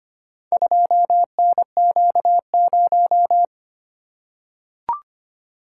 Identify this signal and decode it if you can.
Morse code, heard as a keyed tone, about 25 wpm, 700 Hz. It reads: 2NQ0
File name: practice / 25wpm / Four Letter-Number Combinations - 25wpm